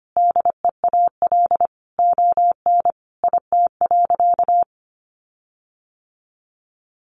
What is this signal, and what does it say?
Morse code, heard as a keyed tone, about 25 wpm, 700 Hz. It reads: DEAL ON IT.
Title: Sentences from Top 1300 Words 25wpm